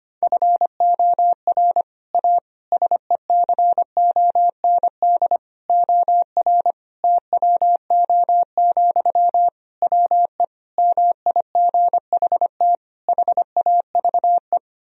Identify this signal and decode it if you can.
Morse code, heard as a keyed tone, about 25 wpm, 700 Hz. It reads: FOR A SECOND OR TWO, WE MIGHT HAVE
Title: Sentences from Top 1000 Words - Review Part 1 - 25wpm